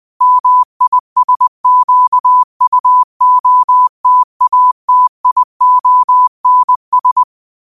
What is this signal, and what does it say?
Morse code, heard as a keyed tone, about 20 wpm, 1000 Hz. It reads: MISQUOTATIONS